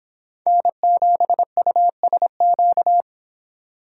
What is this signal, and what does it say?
Morse code, heard as a keyed tone, about 26 wpm, 700 Hz. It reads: N7USQ